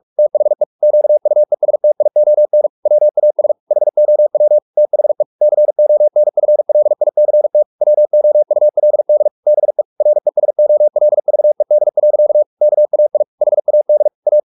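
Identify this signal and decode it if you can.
Morse code, heard as a keyed tone, about 44 wpm, 600 Hz. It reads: THE QUESTION WAS HOW THE CONFLICT WOULD BE RESOLVED. KAI HAD A
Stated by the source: Morse practice at 45 wpm, the novel Never by Ken Follett in English, from CW med Gnister.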